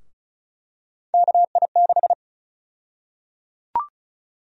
Morse code, heard as a keyed tone, about 34 wpm, 700 Hz. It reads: KI6
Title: USA Call Sign Prefixes - 35wpm